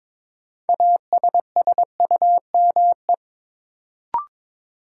Morse code, heard as a keyed tone, about 22 wpm, 700 Hz. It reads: ASSUME
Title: Sentences from Top 1500 Words - New Word Drill Alt - 22wpm